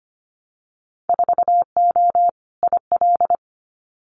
Morse code, heard as a keyed tone, about 25 wpm, 700 Hz. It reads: 4O IL
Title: Winter Field Day Exchanges Encore 25wpm